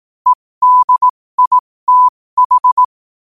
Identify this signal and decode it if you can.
Morse code, heard as a keyed tone, about 18 wpm, spaced as for 15 wpm, 1000 Hz. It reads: EDITH
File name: qotdfiles / 15wpm